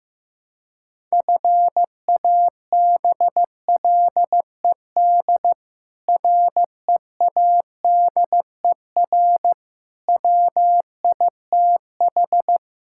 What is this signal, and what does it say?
Morse code, heard as a keyed tone, about 15 wpm, 700 Hz. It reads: FABLED READER WITH
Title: Pangram Challenge v2 - 15wpm